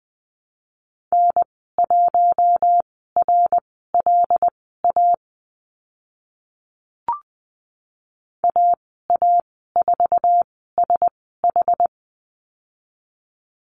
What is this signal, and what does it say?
Morse code, heard as a keyed tone, about 20 wpm, spaced as for 14 wpm, 700 Hz. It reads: N1RLA AA4SH
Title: 082 Lesson 14g - Number 2 - Call Signs 20wpm